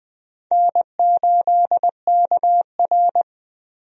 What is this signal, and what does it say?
Morse code, heard as a keyed tone, about 20 wpm, 700 Hz. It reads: N8KR
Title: USA Call Signs - 4 Characters - 20wpm